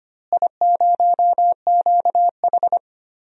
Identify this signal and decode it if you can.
Morse code, heard as a keyed tone, about 25 wpm, 700 Hz. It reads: I0QH
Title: DXCC-100 Call Signs - 4 Characters - Repeated 3x - 25wpm